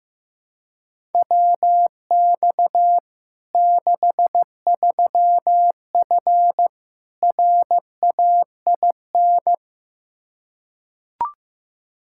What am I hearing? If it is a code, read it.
Morse code, heard as a keyed tone, about 15 wpm, 700 Hz. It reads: WX 63F RAIN
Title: Practice QSO - Exchange 2 - Info - 15wpm